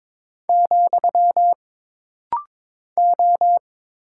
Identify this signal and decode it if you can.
Morse code, heard as a keyed tone, about 22 wpm, 700 Hz. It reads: , O
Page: This rapid-fire timing is geared to push your ICR (Instant Character Recognition) skills.